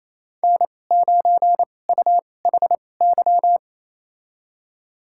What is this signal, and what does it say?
Morse code, heard as a keyed tone, about 28 wpm, spaced as for 20 wpm, 700 Hz. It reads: N9UHY
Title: USA Call Signs - 5 Characters - 2x Character Spacing - 28wpm